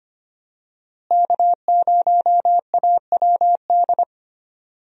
Morse code, heard as a keyed tone, about 25 wpm, 700 Hz. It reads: K0AWD
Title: USA Call Signs - 5 Characters - Repeated 2x - 25wpm